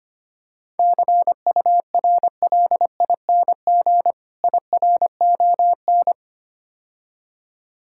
Morse code, heard as a keyed tone, about 26 wpm, 700 Hz. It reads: CURLING IRON
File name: practice / 25wpm / Difficult Words - Speed-Racing with Hints - 25wpm